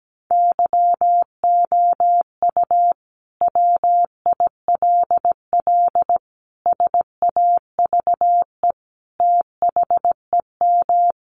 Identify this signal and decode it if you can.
Morse code, heard as a keyed tone, about 17 wpm, 700 Hz. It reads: YOU WILL SAVE THEM
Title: Sentences from Top 1100 Words 17wpm